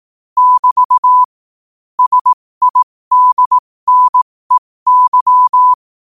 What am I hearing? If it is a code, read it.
Morse code, heard as a keyed tone, about 18 wpm, spaced as for 15 wpm, 1000 Hz. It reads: <BT> SIDNEY
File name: qotdfiles / 15wpm